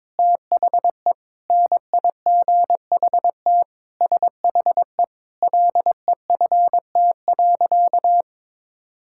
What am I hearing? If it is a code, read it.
Morse code, heard as a keyed tone, about 22 wpm, 700 Hz. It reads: THE NIGHT SHE LEFT.
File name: Sentences with 4-Word Phrases - 22wpm